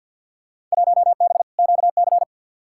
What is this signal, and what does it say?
Morse code, heard as a keyed tone, about 47 wpm, 700 Hz. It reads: 1B QC
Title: Field Day Exchanges Encore - 50wpm